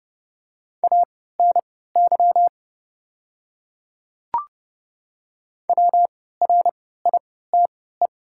Morse code, heard as a keyed tone, about 28 wpm, spaced as for 17 wpm, 690 Hz. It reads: ANY WRITE
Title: 3x Character Spacing - Top 200 Words - 30wpm